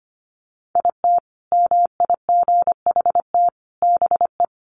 Morse code, heard as a keyed tone, about 25 wpm, 700 Hz. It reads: IT MIGHT BE
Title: Sentences from Top 1200 Words 25wpm